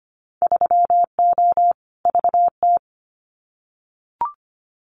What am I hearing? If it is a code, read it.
Morse code, heard as a keyed tone, about 25 wpm, 700 Hz. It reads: 3O VT
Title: Winter Field Day Exchanges Encore 25wpm